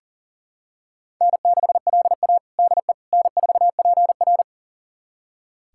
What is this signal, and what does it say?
Morse code, heard as a keyed tone, about 40 wpm, 700 Hz. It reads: N6LA DE N4PR
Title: USA Callsign DE Callsign - 4 Characters - 40wpm